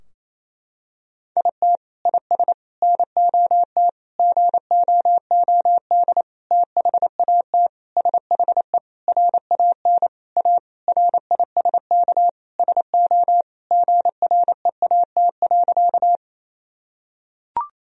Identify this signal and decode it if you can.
Morse code, heard as a keyed tone, about 28 wpm, 700 Hz. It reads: IT IS NOT GOOD THAT SHE RAN A RISK SO GREAT.
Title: Sentences from Top 1000 Words - Review Part 4 - 28wpm